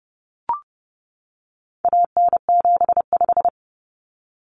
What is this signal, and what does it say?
Morse code, heard as a keyed tone, about 30 wpm, 700 Hz. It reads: AN75
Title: Common Cut Numbers - 4-Digit 30wpm